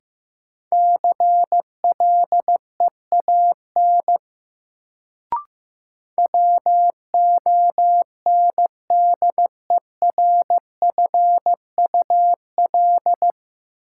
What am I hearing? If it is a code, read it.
Morse code, heard as a keyed tone, about 15 wpm, 700 Hz. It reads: CLEAN WONDERFUL
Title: Sentences from Top 1100 Words - New Word Drill - 15wpm